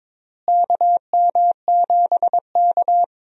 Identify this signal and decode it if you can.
Morse code, heard as a keyed tone, about 22 wpm, 700 Hz. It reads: KM7K
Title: USA Call Signs - 4 Characters - Repeated 3x - 22wpm